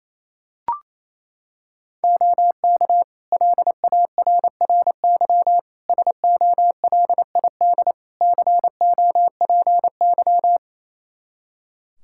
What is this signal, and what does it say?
Morse code, heard as a keyed tone, about 28 wpm, 700 Hz. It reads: OK LARRY SOLID COPY
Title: Practice QSO - Exchange 3 - More Info - 28wpm